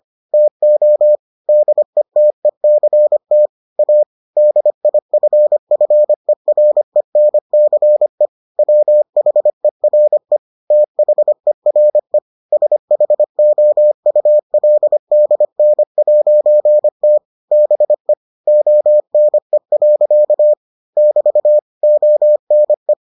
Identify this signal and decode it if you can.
Morse code, heard as a keyed tone, about 25 wpm, 600 Hz. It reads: TO DETECT A DIFFERENCE WHERE THERE SHOULDN'T BE ONE. <BT> ONE